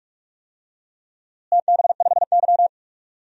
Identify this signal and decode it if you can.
Morse code, heard as a keyed tone, about 46 wpm, 700 Hz. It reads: TB5Y
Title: DXCC-100 Call Signs - 4 Characters - Repeated 3x - 45wpm